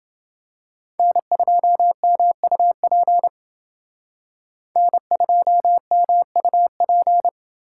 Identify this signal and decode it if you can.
Morse code, heard as a keyed tone, about 28 wpm, 700 Hz. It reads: N2MUP N2MUP